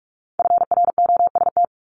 Morse code, heard as a keyed tone, about 46 wpm, 700 Hz. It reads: FROST
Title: Multi-letter Phonemes Repeated 3x 45wpm